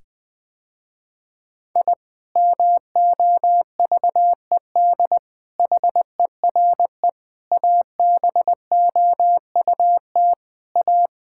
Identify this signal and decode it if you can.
Morse code, heard as a keyed tone, about 20 wpm, 700 Hz. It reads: I MOVED HERE ABOUT A
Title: Sentences with 7-Word Phrases - Spoken Beginning - 20wpm